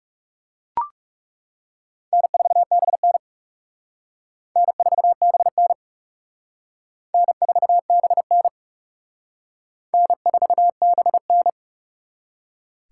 Morse code, heard as a keyed tone, about 36 wpm, 700 Hz. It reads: N4BN N4BN N4BN N4BN
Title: Speed Racing - USA 1x2 Call Signs 30wpm